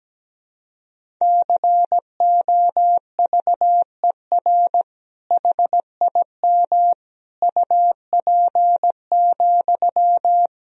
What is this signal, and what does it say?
Morse code, heard as a keyed tone, about 17 wpm, 700 Hz. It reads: COVER HIM UP,
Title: Sentences from Top 1000 Words - Review Part 1 - 17wpm